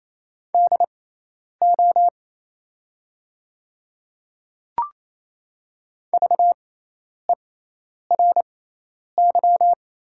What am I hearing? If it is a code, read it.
Morse code, heard as a keyed tone, about 28 wpm, spaced as for 10 wpm, 690 Hz. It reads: DO VERY
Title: 6x Character Spacing - Top 500 Words - 28wpm